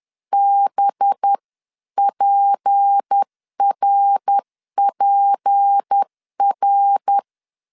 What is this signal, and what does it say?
Morse code, heard as a keyed tone, about 10 wpm, 800 Hz. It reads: BPRPR
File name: CW / DET / 1 6 1500C S C 10wpm 4252